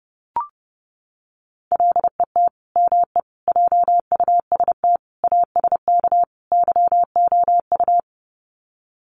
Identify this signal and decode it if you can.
Morse code, heard as a keyed tone, about 30 wpm, 700 Hz. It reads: LET ME JUST ASK YOU
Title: Sets of 5 Words Encore2 30wpm